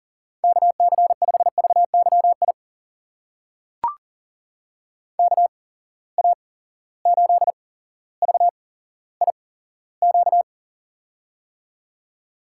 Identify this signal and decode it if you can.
Morse code, heard as a keyed tone, about 40 wpm, spaced as for 17 wpm, 700 Hz. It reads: KC5VYI KA8VIQ
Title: USA Call Signs - 6 Characters - 8x Character Spacing - 40wpm